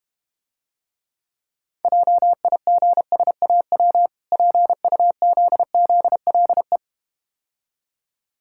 Morse code, heard as a keyed tone, about 32 wpm, 700 Hz. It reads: JIGSAW PUZZLE